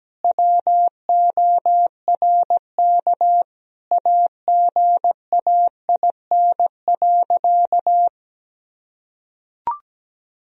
Morse code, heard as a keyed tone, about 17 wpm, 700 Hz. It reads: WORK AGAIN.
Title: Sentences from Top 1000 Words - Review Part 2 - 17wpm